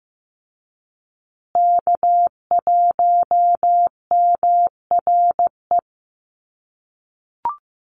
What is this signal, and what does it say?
Morse code, heard as a keyed tone, about 15 wpm, 700 Hz. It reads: K1MRE
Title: USA Call Signs Repeated 2x 15wpm